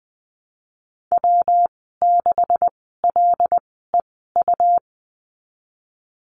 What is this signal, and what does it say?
Morse code, heard as a keyed tone, about 20 wpm, spaced as for 14 wpm, 700 Hz. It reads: W6LEU
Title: 245 Lesson 39g - Letter z - Call Signs 20wpm